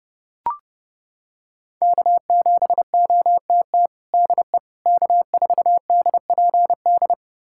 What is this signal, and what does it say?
Morse code, heard as a keyed tone, about 30 wpm, 700 Hz. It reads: K7OTT DE K4DPD
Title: USA Callsign DE Callsign - 5 Characters - 30wpm